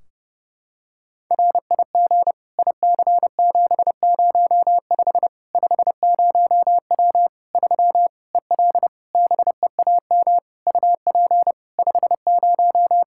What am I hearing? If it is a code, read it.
Morse code, heard as a keyed tone, about 30 wpm, 700 Hz. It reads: RIG IC705 50W 3 EL BEAM UP 50
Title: Practice QSO - Exchange 2 - Info - 30wpm